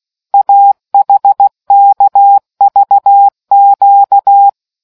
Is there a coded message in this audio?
Yes: Morse code from 0.3 to 4.5 s, heard as a keyed tone, about 16 wpm, 800 Hz. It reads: AHKVQ